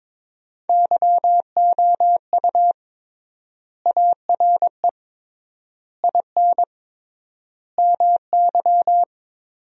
Morse code, heard as a keyed tone, about 22 wpm, 700 Hz. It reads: YOU ARE IN MY
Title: 3x Word-Spacing - Sentences from Top 500 Words - 22wpm